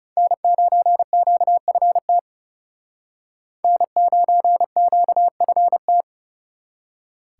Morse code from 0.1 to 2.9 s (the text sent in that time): N9QFT